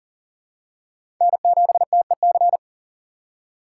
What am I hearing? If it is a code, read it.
Morse code, heard as a keyed tone, about 37 wpm, 700 Hz. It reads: N7TEC
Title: USA Call Signs - 5 Characters - 2x Character Spacing - 40wpm